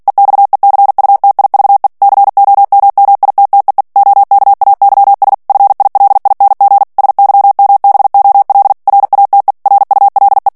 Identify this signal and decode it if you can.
Morse code, heard as a keyed tone, about 48 wpm, 800 Hz. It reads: EXECUTIVE COMMITTEE OKAYS FILING SYMBOL RATE RULE
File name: ANGLAIS texte clair 3 vitesse 45 mots minute